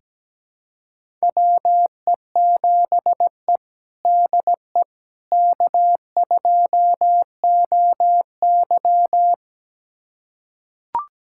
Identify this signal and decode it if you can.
Morse code, heard as a keyed tone, about 17 wpm, 700 Hz. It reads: WE7E DE K2OY
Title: USA Callsign DE Callsign - 4 Characters - 17wpm